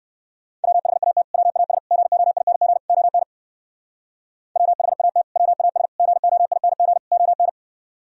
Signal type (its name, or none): Morse code